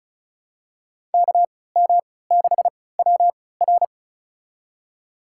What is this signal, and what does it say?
Morse code, heard as a keyed tone, about 34 wpm, spaced as for 20 wpm, 690 Hz. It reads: KM6WR
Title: USA Call Signs - 3x Character Spacing - 35wpm